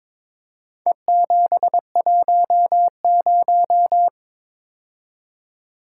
Morse code, heard as a keyed tone, about 22 wpm, 700 Hz. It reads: E710
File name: practice / 22wpm / Four Letter-Number Combinations - 22wpm